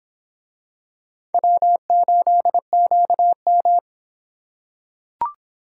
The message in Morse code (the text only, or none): W8QM